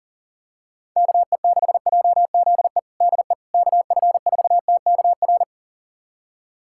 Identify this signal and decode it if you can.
Morse code, heard as a keyed tone, about 40 wpm, 690 Hz. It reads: KE6JZE DE KF4TKR
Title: USA Callsign DE Callsign - 6 Characters - 40wpm